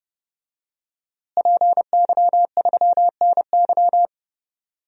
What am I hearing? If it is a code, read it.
Morse code, heard as a keyed tone, about 30 wpm, 700 Hz. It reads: PY3NY
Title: DXCC-100 Call Signs - 5 Characters - Repeated 2x - 30wpm